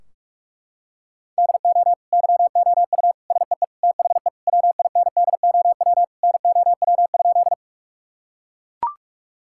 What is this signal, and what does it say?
Morse code, heard as a keyed tone, about 43 wpm, 700 Hz. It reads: DO YOU SEE THE WINDOW NOW?
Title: Sentences from Top 1600 Words 45wpm